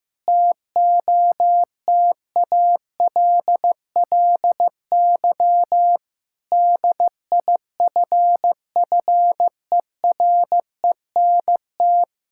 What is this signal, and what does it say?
Morse code, heard as a keyed tone, about 15 wpm, 700 Hz. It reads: TOTALLY DIFFERENT
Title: Sentences from Top 1600 Words 15wpm